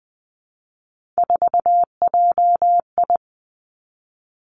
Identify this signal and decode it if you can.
Morse code, heard as a keyed tone, about 20 wpm, 700 Hz. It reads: 4JI
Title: Three Challenging Letter-Number Combinations 20wpm